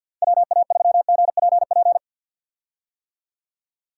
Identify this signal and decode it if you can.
Morse code, heard as a keyed tone, about 52 wpm, 700 Hz. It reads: WA3GPP
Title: K1USN SST QSO Practice - 50wpm